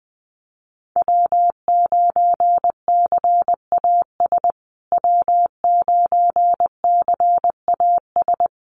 Morse code, heard as a keyed tone, about 20 wpm, 700 Hz. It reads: W9CAS W9CAS
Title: USA Call Signs Repeated 2x 20wpm